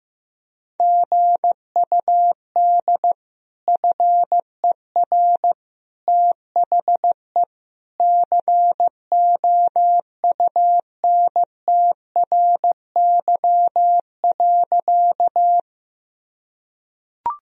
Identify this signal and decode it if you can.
Morse code, heard as a keyed tone, about 15 wpm, 700 Hz. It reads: GUD FER THE COUNTRY.
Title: Sentences from Top 300 Words and Abbreviations 15wpm